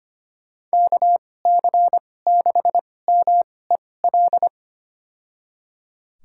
Morse code, heard as a keyed tone, about 25 wpm, spaced as for 18 wpm, 700 Hz. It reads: KC6MEL
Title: USA Call Signs - 6 Characters - 2x Character Spacing - 25wpm